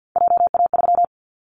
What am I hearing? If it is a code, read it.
Morse code, heard as a keyed tone, about 50 wpm, 700 Hz. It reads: JA3
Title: Callsign Prefixes for Top 25 DXCC - 50wpm